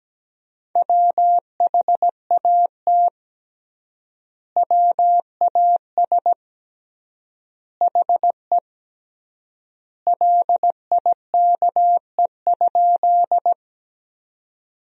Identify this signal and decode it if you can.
Morse code, heard as a keyed tone, about 17 wpm, 700 Hz. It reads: WHAT WAS HE LIKE?
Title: 3x Word-Spacing - Sentences from Top 100 Words - 17wpm